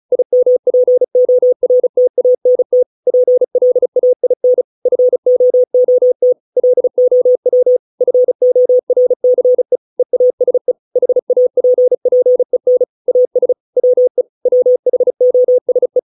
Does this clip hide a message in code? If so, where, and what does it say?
Yes: Morse code from 0.1 to 16.0 s, heard as a keyed tone, about 34 wpm, 500 Hz. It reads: IMPORTANT PLAIN FOOT LOW FORCE EASE HAPPEN AS WE WHOSE